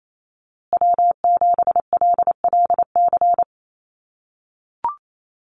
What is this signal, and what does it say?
Morse code, heard as a keyed tone, about 28 wpm, 700 Hz. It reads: W7LLC
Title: USA Call Signs Repeated 2x 28wpm